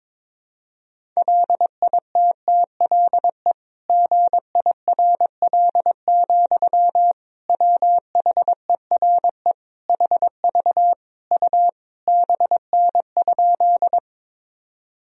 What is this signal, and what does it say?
Morse code, heard as a keyed tone, about 22 wpm, 700 Hz. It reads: LITTLE GIRL, WHERE HV U BN?
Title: Sentences from Top 400 Words and Abbreviations 22wpm